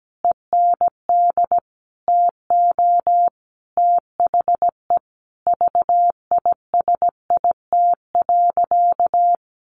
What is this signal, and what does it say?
Morse code, heard as a keyed tone, about 17 wpm, 700 Hz. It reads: END TO THE VISIT.
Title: Sentences from Top 1200 Words 17wpm